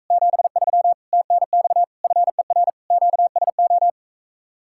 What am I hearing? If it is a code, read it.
Morse code, heard as a keyed tone, about 45 wpm, 700 Hz. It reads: 73 TNX FER QSO